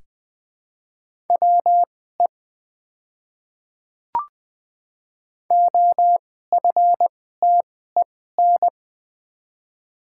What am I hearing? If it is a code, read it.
Morse code, heard as a keyed tone, about 20 wpm, spaced as for 14 wpm, 700 Hz. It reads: WE OFTEN
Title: 2x Character Spacing - Top 300 Words - 20wpm